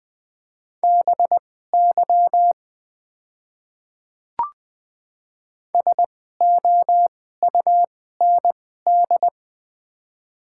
Morse code, heard as a keyed tone, about 20 wpm, spaced as for 15 wpm, 700 Hz. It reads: BY SOUND
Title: 2x Character Spacing - Top 200 Words - 20wpm